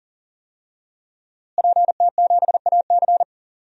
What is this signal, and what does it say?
Morse code, heard as a keyed tone, about 40 wpm, 700 Hz. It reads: PT7AC
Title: DXCC-100 Call Signs - 5 Characters - Repeated 3x - 40wpm